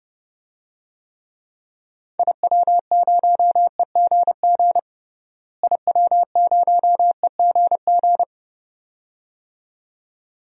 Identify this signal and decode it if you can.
Morse code, heard as a keyed tone, about 32 wpm, 700 Hz. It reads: IW0EGG IW0EGG
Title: DXCC-100 Call Signs - 6 Characters - Repeated 2x - 30wpm